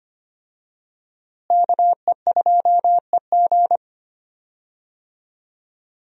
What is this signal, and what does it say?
Morse code, heard as a keyed tone, about 25 wpm, 700 Hz. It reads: KE2EG
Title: USA Call Signs - 5 Characters - Repeated 2x - 25wpm